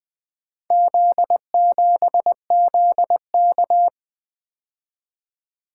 Morse code, heard as a keyed tone, about 20 wpm, 700 Hz. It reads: Z7ZK
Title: Four Letter-Number Combinations - 20wpm